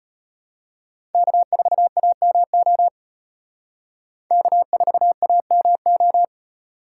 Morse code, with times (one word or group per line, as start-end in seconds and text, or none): 1.1-2.9 K4AMO
4.3-6.2 K4AMO